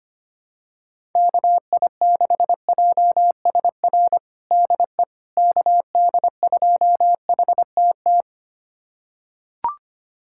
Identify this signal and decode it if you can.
Morse code, heard as a keyed tone, about 25 wpm, 700 Hz. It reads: KI6JSR DE KD2HTT
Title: USA Callsign DE Callsign - 6 Characters - 25wpm